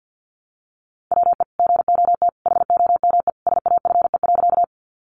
Morse code, heard as a keyed tone, about 50 wpm, 700 Hz. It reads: WE GOT HOME SAFE.
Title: Sentences from Top 900 Words 50wpm